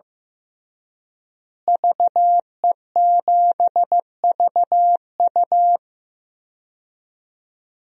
Morse code, heard as a keyed tone, about 15 wpm, 700 Hz. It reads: VE7VU
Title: DXCC-100 Call Signs - 5 Characters - Repeated 3x - 15wpm